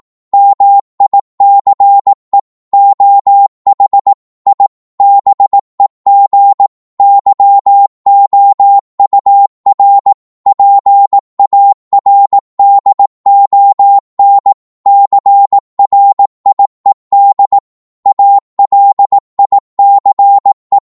Morse code, heard as a keyed tone, about 18 wpm, 800 Hz. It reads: MICE OH I BEG YOUR PARDON CRIED ALICE
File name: Morse / psg-18wpm